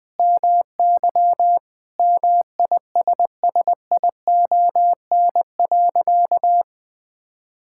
Morse code, heard as a keyed tone, about 21 wpm, 700 Hz. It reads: MY MISSION.
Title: Sentences from Top 1600 Words 20wpm